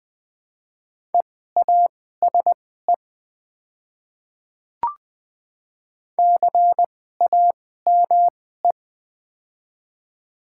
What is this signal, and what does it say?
Morse code, heard as a keyed tone, about 20 wpm, spaced as for 14 wpm, 700 Hz. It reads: EASE CAME
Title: 2x Character Spacing - Top 300 Words - 20wpm